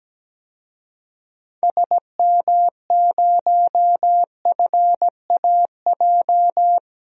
Morse code, heard as a keyed tone, about 17 wpm, 700 Hz. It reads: SM0FAJ
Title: DXCC-100 Call Signs - 6 Characters - 17wpm